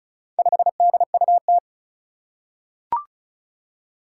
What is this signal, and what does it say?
Morse code, heard as a keyed tone, about 36 wpm, 700 Hz. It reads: 5DUT
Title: Four Challenging Letter-Number Combinations 35wpm